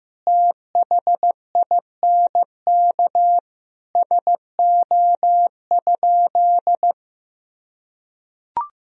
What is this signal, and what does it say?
Morse code, heard as a keyed tone, about 15 wpm, 700 Hz. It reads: THINK SO?
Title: Sentences from Top 300 Words and Abbreviations 15wpm